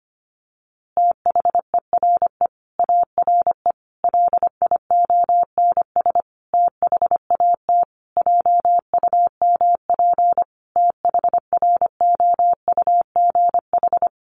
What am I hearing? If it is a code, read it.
Morse code, heard as a keyed tone, about 25 wpm, 700 Hz. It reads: THERE ARE LIONS THAT JUMP THROUGH
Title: Easy Phrases 25wpm